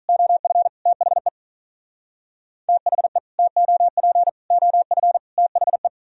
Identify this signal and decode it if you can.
Morse code, heard as a keyed tone, about 44 wpm, 700 Hz. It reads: OF THE THE TOP OF THE
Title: Speed-Racing - Sets of 4 Words 35wpm